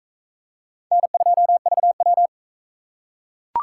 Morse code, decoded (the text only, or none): N2VW